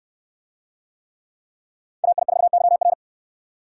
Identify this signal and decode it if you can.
Morse code, heard as a keyed tone, about 69 wpm, 700 Hz. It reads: AI4YR